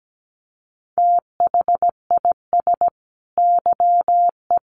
Morse code, heard as a keyed tone, about 17 wpm, 700 Hz. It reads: THIS YE
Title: Sets of 2 Words 17wpm